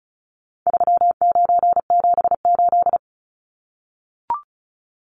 Morse code, heard as a keyed tone, about 34 wpm, 700 Hz. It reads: 3978
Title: Numbers - 4-Digit 35wpm